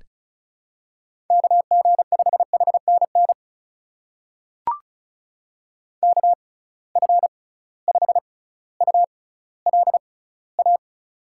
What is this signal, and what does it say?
Morse code, heard as a keyed tone, about 36 wpm, spaced as for 18 wpm, 700 Hz. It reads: KG5HNN KF5ULA
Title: USA Call Signs - 6x Character Spacing - 35wpm